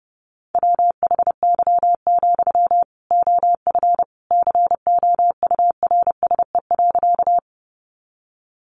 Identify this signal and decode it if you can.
Morse code, heard as a keyed tone, about 28 wpm, 700 Hz. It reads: WHY, OF COURSE.
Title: Sentences from Top 500 Words 30wpm